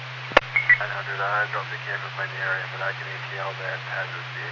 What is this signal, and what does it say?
Emergency Alert System (SAME) tones, heard as harsh buzzing bursts, with speech. It is heard from 0.5 to 4.1 s.